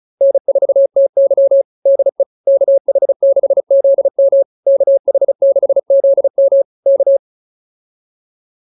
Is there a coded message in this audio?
Yes: Morse code from 0.2 to 7.2 s, heard as a keyed tone, about 34 wpm, 560 Hz. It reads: N4TY DE KH6ZM KH6ZM K